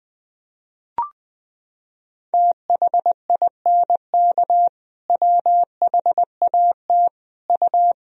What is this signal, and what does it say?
Morse code, heard as a keyed tone, about 21 wpm, 700 Hz. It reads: THINK WHAT U
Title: Sentences from Top 200 Words and Abbreviations 20wpm